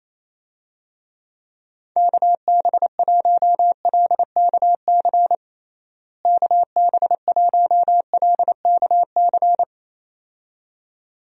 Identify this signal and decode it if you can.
Morse code, heard as a keyed tone, about 27 wpm, 700 Hz. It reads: KB1LKC KB1LKC
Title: USA Call Signs - 6 Characters - Repeated 2x - 28wpm